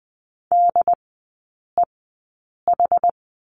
Morse code, heard as a keyed tone, about 20 wpm, spaced as for 8 wpm, 700 Hz. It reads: DEH
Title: 074 Lesson 13e - Letter L - Three Characters 20wpm